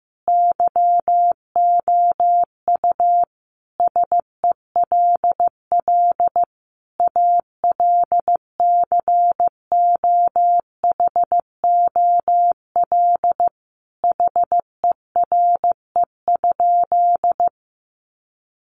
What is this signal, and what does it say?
Morse code, heard as a keyed tone, about 15 wpm, 700 Hz. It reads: YOU SELL ALCOHOL HERE?
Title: Sentences from Top 1400 Words 15wpm